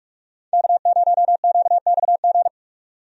Morse code, heard as a keyed tone, about 46 wpm, 700 Hz. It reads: K0QXG
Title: USA Call Signs - 5 Characters - 45wpm